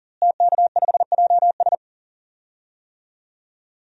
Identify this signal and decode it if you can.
Morse code, heard as a keyed tone, about 38 wpm, 700 Hz. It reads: TK5JS
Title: DXCC-100 Call Signs - 5 Characters - Repeated 3x - 40wpm